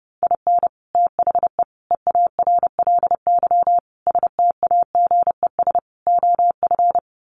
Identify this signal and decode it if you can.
Morse code, heard as a keyed tone, about 28 wpm, 700 Hz. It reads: IN THE EARLY STAGES OF
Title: Sets of 5 Words Encore 30wpm